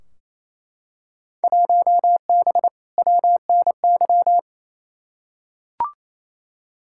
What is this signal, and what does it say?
Morse code, heard as a keyed tone, about 28 wpm, 700 Hz. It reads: 1B WNY
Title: Field Day Exchanges Encore - 28wpm